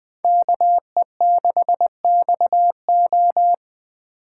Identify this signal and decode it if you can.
Morse code, heard as a keyed tone, about 20 wpm, 700 Hz. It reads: KE6XO
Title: Speed Racing - USA 2x2 Call Signs 17wpm